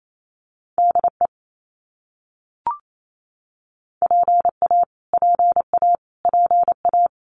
Morse code, heard as a keyed tone, about 28 wpm, 700 Hz. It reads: DE PA PA PA
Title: US States Repeated 3x 28wpm